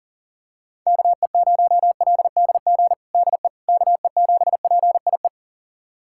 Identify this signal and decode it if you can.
Morse code, heard as a keyed tone, about 40 wpm, 700 Hz. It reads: KE0LDG DE KE7PIE
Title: USA Callsign DE Callsign - 6 Characters - 40wpm